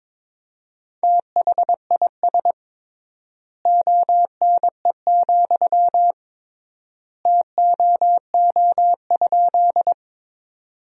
Morse code, heard as a keyed tone, about 22 wpm, 700 Hz. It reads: THIS ONE, TOO?
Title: 3x Word-Spacing - Sentences from Top 200 Words - 22wpm